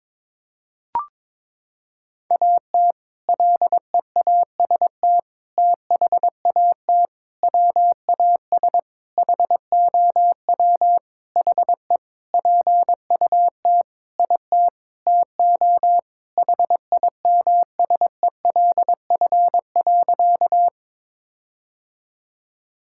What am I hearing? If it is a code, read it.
Morse code, heard as a keyed tone, about 22 wpm, 700 Hz. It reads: AT LEAST THAT WAS HOW HE PUT IT TO HIMSELF.
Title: Sentences from Top 1600 Words 22wpm